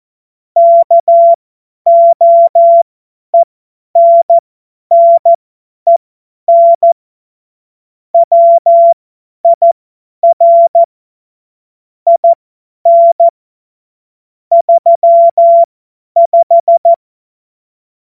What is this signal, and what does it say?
Morse code, heard as a keyed tone, about 14 wpm, spaced as for 10 wpm, 680 Hz. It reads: KOENNEN WIR IN 35
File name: Morsekurs / Witze / Witz 25. Juli 2023 50 F70